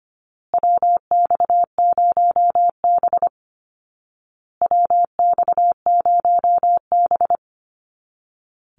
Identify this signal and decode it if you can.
Morse code, heard as a keyed tone, about 25 wpm, 700 Hz. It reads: WX0B WX0B
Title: Call Signs Repeated 2x - 25wpm